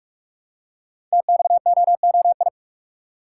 Morse code, heard as a keyed tone, about 44 wpm, 700 Hz. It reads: TXOOI
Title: Code Groups - 5 Letters - 45wpm